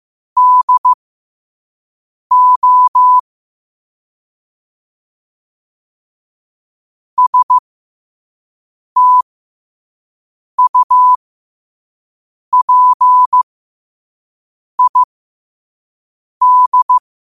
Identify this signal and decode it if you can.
Morse code, heard as a keyed tone, about 15 wpm, spaced as for 5 wpm, 1000 Hz. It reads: DO STUPID
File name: qotdfiles / 5wpm